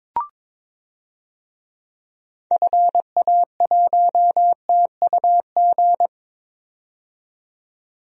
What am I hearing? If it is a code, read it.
Morse code, heard as a keyed tone, about 22 wpm, 700 Hz. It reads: FA1TUG
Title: DXCC-100 Call Signs - 6 Characters - 22wpm